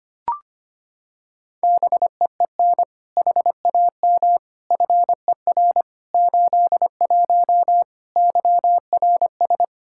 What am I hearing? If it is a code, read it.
Morse code, heard as a keyed tone, about 25 wpm, 700 Hz. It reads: BEEN HAM FER 81 YRS
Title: Practice QSO - Exchange 3 - More Info - 25wpm